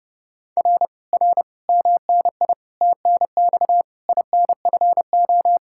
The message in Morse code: R R MNI TNX INFO

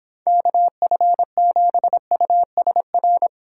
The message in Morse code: KF7USR

